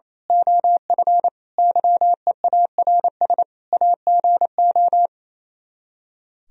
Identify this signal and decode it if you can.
Morse code, heard as a keyed tone, about 27 wpm, 700 Hz. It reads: OF YEARS AGO
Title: Speed-Racing - Sets of 5 Words 28wpm